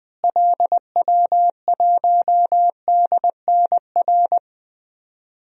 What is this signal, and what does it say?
Morse code, heard as a keyed tone, about 20 wpm, 700 Hz. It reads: LW1DNR
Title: DXCC-100 Call Signs - 6 Characters - Repeated 2x - 20wpm